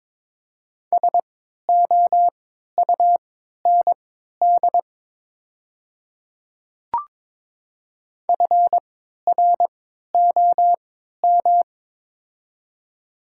Morse code, heard as a keyed tone, about 22 wpm, spaced as for 12 wpm, 700 Hz. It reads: SOUND FROM
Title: 3x Character Spacing - Top 200 Words - 22wpm